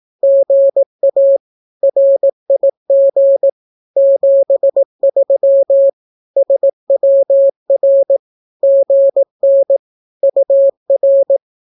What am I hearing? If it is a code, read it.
Morse code, heard as a keyed tone, about 18 wpm, 550 Hz. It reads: GA RIG 73 SWR GN UR
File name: naukapodstaw / skroty / skroty